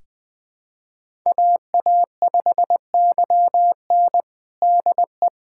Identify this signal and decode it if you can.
Morse code, heard as a keyed tone, about 20 wpm, 700 Hz. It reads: AA5YN DE
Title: USA Callsign DE Callsign - 5 Characters - 20wpm